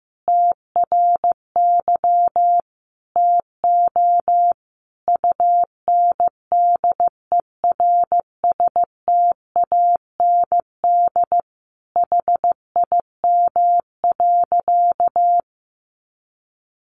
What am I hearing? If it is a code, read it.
Morse code, heard as a keyed tone, about 15 wpm, 700 Hz. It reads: TRY TO UNDERSTAND HIM.
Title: Sentences from Top 600 Words 15wpm